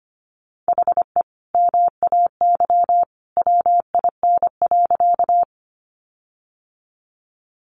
Morse code, heard as a keyed tone, about 25 wpm, 700 Hz. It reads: HE MAY WIN.
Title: Sentences from Top 1500 Words 25wpm